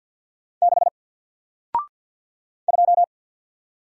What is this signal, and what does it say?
Morse code, heard as a keyed tone, about 53 wpm, 700 Hz. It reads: 6 2
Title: Single Letter-Number 50wpm